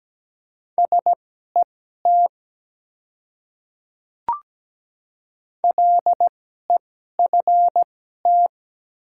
Morse code, heard as a keyed tone, about 17 wpm, spaced as for 12 wpm, 690 Hz. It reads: SET LEFT
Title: 2x Character Spacing - Top 300 Words - 17wpm